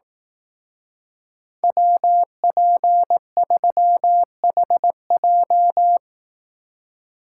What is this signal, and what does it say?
Morse code, heard as a keyed tone, about 18 wpm, 700 Hz. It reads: WP3HJ
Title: Speed Racing - USA 2x2 Call Signs 15wpm